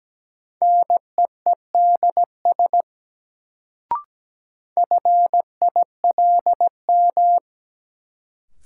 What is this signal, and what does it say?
Morse code, heard as a keyed tone, about 17 wpm, 700 Hz. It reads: NEEDS FILM
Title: Sentences from Top 1000 Words - Review Part 3 - New Word Drill - 17wpm